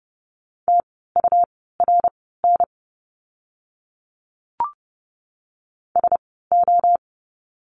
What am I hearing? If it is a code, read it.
Morse code, heard as a keyed tone, about 30 wpm, spaced as for 17 wpm, 700 Hz. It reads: TURN SO
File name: course / 30:10wpm / 230 Lesson 37f - Letter x - Words 30wpm